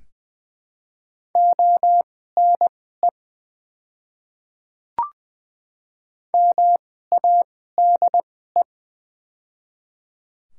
Morse code, heard as a keyed tone, about 20 wpm, spaced as for 15 wpm, 700 Hz. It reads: ONE MADE
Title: 2x Character Spacing - Top 200 Words - 20wpm